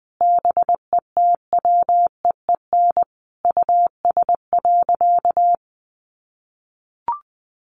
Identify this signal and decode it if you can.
Morse code, heard as a keyed tone, about 20 wpm, 700 Hz. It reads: BETWEEN US.
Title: Sentences from Top 800 Words 20wpm